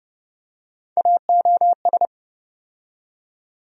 Morse code, heard as a keyed tone, about 30 wpm, 700 Hz. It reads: AOS
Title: USA Call Sign Postfixes - 3 Letters 30wpm